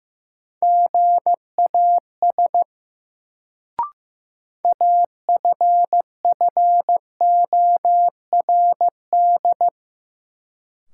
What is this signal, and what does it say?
Morse code, heard as a keyed tone, about 15 wpm, 700 Hz. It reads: GAS AFFORD
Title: Sentences from Top 1100 Words - New Word Drill Alt - 15wpm